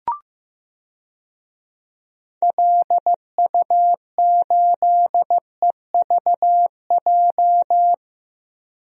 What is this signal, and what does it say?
Morse code, heard as a keyed tone, about 16 wpm, 700 Hz. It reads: LU8EVJ
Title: DXCC-100 Call Signs - 6 Characters - Repeated 2x - 15wpm